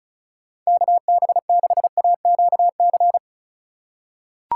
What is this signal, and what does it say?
Morse code, heard as a keyed tone, about 36 wpm, 700 Hz. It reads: KB6AQC
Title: USA Call Signs - 6 Characters - 3x Character Spacing - 35wpm